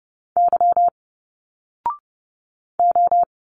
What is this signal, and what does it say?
Morse code, heard as a keyed tone, about 30 wpm, 700 Hz. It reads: Y O
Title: Single Letters 30wpm